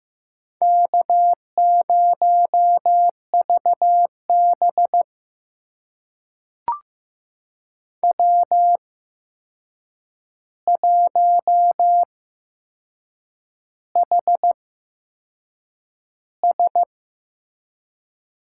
Morse code, heard as a keyed tone, about 15 wpm, spaced as for 6 wpm, 700 Hz. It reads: K0VB W1HS
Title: USA Call Signs - 4 Characters - 8x Character Spacing - 15wpm